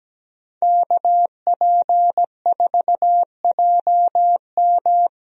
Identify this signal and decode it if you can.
Morse code, heard as a keyed tone, about 17 wpm, 700 Hz. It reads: KP4JM
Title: DXCC-100 Call Signs - 5 Characters - Repeated 3x - 17wpm